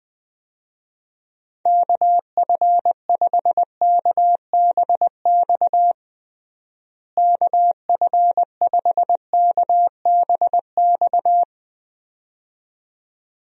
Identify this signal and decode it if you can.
Morse code, heard as a keyed tone, about 21 wpm, 700 Hz. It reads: KF5KBX KF5KBX
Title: USA Call Signs - 6 Characters - Repeated 2x - 20wpm